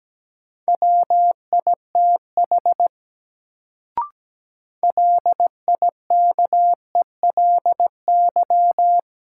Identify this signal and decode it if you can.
Morse code, heard as a keyed tone, about 17 wpm, 700 Hz. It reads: WITH LIKELY
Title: Sets of 2 Words - New Word Drill - 17wpm